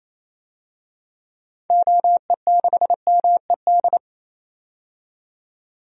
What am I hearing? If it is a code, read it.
Morse code, heard as a keyed tone, about 28 wpm, 700 Hz. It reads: OE6MED
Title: DXCC-100 Call Signs - 6 Characters - 28wpm